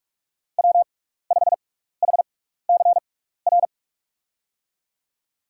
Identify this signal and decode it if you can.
Morse code, heard as a keyed tone, about 46 wpm, spaced as for 15 wpm, 700 Hz. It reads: W5HCR
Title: USA Call Signs - 6x Character Spacing - 45wpm